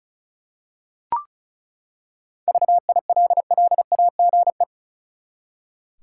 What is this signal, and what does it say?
Morse code, heard as a keyed tone, about 34 wpm, 700 Hz. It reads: VILLAGE
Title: Seven Letter Words - 35wpm